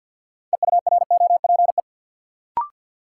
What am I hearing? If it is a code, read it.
Morse code, heard as a keyed tone, about 53 wpm, 700 Hz. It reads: EUROPE
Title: Sets of 5 Words Encore - New Word Drill - 50wpm